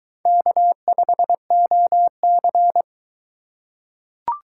Morse code, heard as a keyed tone, about 22 wpm, 700 Hz. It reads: K5OC